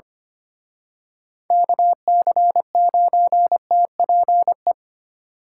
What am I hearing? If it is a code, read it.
Morse code, heard as a keyed tone, about 31 wpm, 700 Hz. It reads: KC9TPE
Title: Speed Racing - USA Call Signs 25wpm